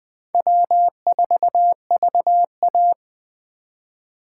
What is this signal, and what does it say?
Morse code, heard as a keyed tone, about 20 wpm, 700 Hz. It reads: W4VA
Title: USA Call Signs - 4 Characters - 3x Character Spacing - 20wpm